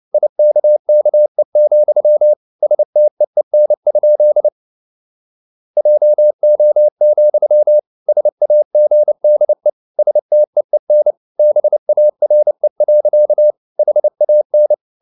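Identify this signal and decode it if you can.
Morse code, heard as a keyed tone, about 29 wpm, 600 Hz. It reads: IKKE, STEEN? JO, SAGDE STEEN BARE. HAN
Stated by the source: Fast (QRQ) Morse at 29 wpm, from CW med Gnister.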